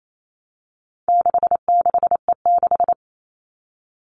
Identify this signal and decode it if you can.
Morse code, heard as a keyed tone, about 28 wpm, 700 Hz. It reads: 66E6
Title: All Cut Numbers - 4-Digit 28wpm